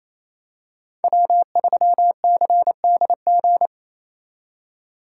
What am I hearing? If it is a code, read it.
Morse code, heard as a keyed tone, about 28 wpm, 700 Hz. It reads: W3CDG
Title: USA Call Signs - 2x Character Spacing - 28wpm